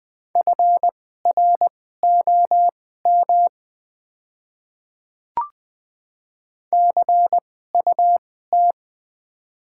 Morse code, heard as a keyed tone, about 20 wpm, spaced as for 14 wpm, 700 Hz. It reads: FROM CUT
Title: 2x Character Spacing - Top 300 Words - 20wpm